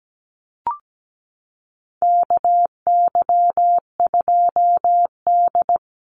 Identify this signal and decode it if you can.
Morse code, heard as a keyed tone, about 17 wpm, 700 Hz. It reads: KY2D
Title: Call Signs - 17wpm